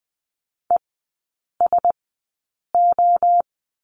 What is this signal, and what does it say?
Morse code, heard as a keyed tone, about 20 wpm, spaced as for 8 wpm, 700 Hz. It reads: ESO
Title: 047 Lesson 9e - Number 4 - Three Characters 20wpm